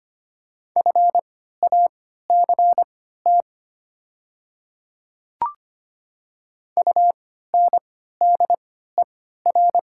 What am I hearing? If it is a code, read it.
Morse code, heard as a keyed tone, about 25 wpm, spaced as for 14 wpm, 700 Hz. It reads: FACT UNDER
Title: 3x Character Spacing - Top 500 Words - 25wpm